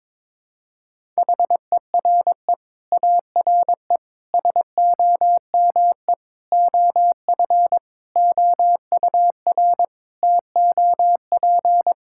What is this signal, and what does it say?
Morse code, heard as a keyed tone, about 22 wpm, 700 Hz. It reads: HERE ARE SOME OF OUR TOP
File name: Sets of 9 Words - 22wpm